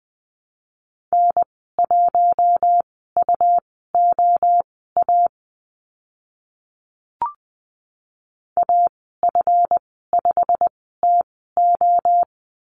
Morse code, heard as a keyed tone, about 20 wpm, spaced as for 14 wpm, 700 Hz. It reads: N1UOA AF5TO
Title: 140 Lesson 23g - Letter f - Call Signs 20wpm